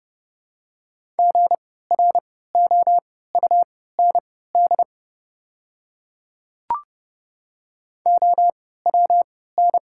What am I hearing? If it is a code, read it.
Morse code, heard as a keyed tone, about 30 wpm, spaced as for 17 wpm, 690 Hz. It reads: GROUND OWN
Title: 3x Character Spacing - Top 500 Words - 30wpm